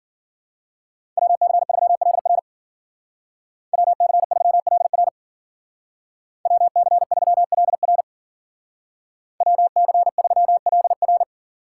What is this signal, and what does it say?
Morse code, heard as a keyed tone, about 52 wpm, 700 Hz. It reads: WC3LR WC3LR WC3LR WC3LR